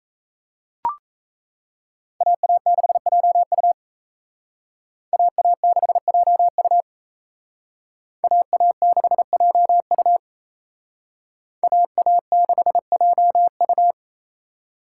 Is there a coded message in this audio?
Yes: Morse code from 2.2 to 13.9 s, heard as a keyed tone, about 35 wpm, 700 Hz. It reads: AA6JU AA6JU AA6JU AA6JU